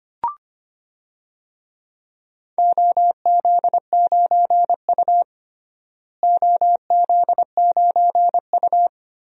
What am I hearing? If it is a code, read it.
Morse code, heard as a keyed tone, about 25 wpm, 700 Hz. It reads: OZ9U OZ9U
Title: DXCC-100 Call Signs - 4 Characters - Repeated 3x - 25wpm